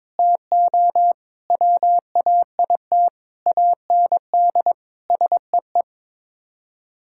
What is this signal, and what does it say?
Morse code, heard as a keyed tone, about 22 wpm, 700 Hz. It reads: TO WAIT AND SEE
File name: practice / 22wpm / Sets of 7 Words - 22wpm